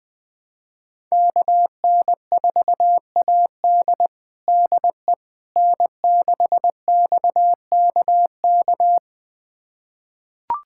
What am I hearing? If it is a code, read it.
Morse code, heard as a keyed tone, about 21 wpm, 700 Hz. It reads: KN4AD DE N6XKK
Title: USA Callsign DE Callsign - 5 Characters - 20wpm